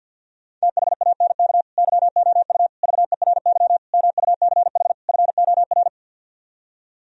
Morse code, heard as a keyed tone, about 62 wpm, 700 Hz. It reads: THANK YOU VERY MUCH FOR
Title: Speed-Racing - Sets of 5 Words 50wpm